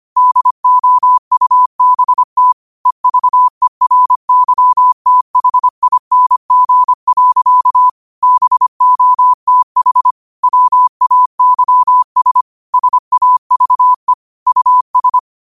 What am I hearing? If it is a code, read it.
Morse code, heard as a keyed tone, about 25 wpm, 1000 Hz. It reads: DOUBT EVERYTHING. BOTH WAYS SAVE US